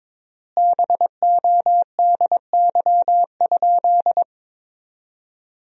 Morse code, heard as a keyed tone, about 22 wpm, 700 Hz. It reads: BODY?
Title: 3x Word-Spacing - Sentences from Top 400 Words - 22wpm